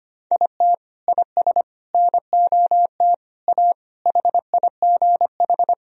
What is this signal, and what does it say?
Morse code, heard as a keyed tone, about 25 wpm, 700 Hz. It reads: IT IS NOT A HIGH
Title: Sentences from Top 1000 Words - Review Part 4 - 25wpm